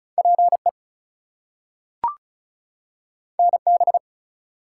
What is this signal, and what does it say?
Morse code, heard as a keyed tone, about 36 wpm, 700 Hz. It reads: PE NB
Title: US States and Canadian Provinces 35wpm